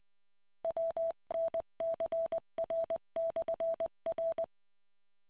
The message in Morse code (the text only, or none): WNCR/R